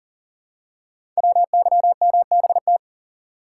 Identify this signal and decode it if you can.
Morse code, heard as a keyed tone, about 42 wpm, 700 Hz. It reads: WYMBT
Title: Code Groups - 5 Letters - 40wpm